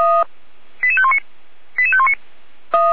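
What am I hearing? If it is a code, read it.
Five-tone selective call (ZVEI), heard as quick runs of five tones, tones of about 70 ms. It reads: 88519 88519
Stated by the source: ZVEI five-tone selective call received with an SDR.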